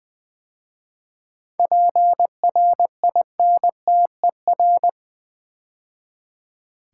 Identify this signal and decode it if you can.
Morse code, heard as a keyed tone, about 20 wpm, 700 Hz. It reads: PRINTER
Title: 7 Letter Words - First 3 Letters Given - 20wpm